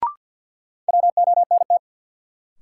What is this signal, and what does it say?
Morse code, heard as a keyed tone, about 52 wpm, 700 Hz. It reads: WONT
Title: Sentences from Top 1000 Words - Review Part 6 - New Word Drill - 50wpm